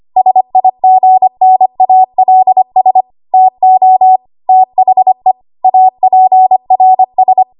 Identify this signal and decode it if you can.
Morse code, heard as a keyed tone, about 25 wpm, 750 Hz. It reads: SIGNALS TO THE APRS
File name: Morse / 25WPM